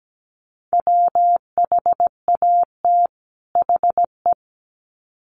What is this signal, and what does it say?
Morse code, heard as a keyed tone, about 17 wpm, 700 Hz. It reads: WHAT HE
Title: Sets of 2 Words 17wpm